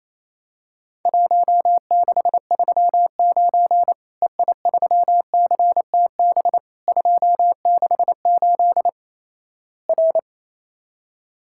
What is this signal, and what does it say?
Morse code, heard as a keyed tone, about 28 wpm, 700 Hz. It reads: 1639 EI3CTB 268 R
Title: Worked All Europe DX Contest - QTC Exchange - 28wpm